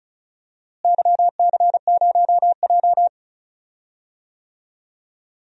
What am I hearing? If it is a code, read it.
Morse code, heard as a keyed tone, about 34 wpm, 700 Hz. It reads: YC0J
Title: DXCC-100 Call Signs - 4 Characters - Repeated 3x - 35wpm